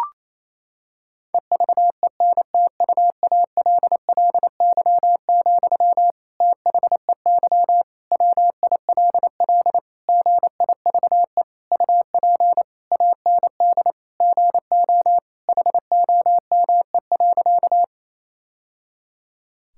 Morse code, heard as a keyed tone, about 28 wpm, 700 Hz. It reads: EVENTUALLY, THEY WILL GIVE UP AND GO HOME.
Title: Sentences from Top 1000 Words - Review Part 6 - 28wpm